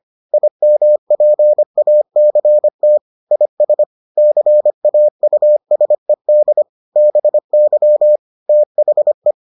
Morse code, heard as a keyed tone, about 25 wpm, 600 Hz. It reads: IMPACT IS CAUSED BY THE